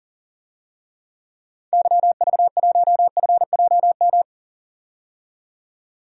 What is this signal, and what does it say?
Morse code, heard as a keyed tone, about 40 wpm, 700 Hz. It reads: YV1FJM
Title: DXCC-100 Call Signs - 6 Characters - 40wpm